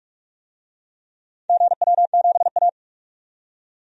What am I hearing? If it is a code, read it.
Morse code, heard as a keyed tone, about 43 wpm, 700 Hz. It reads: GW7A
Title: DXCC-100 Call Signs - 4 Characters - 45wpm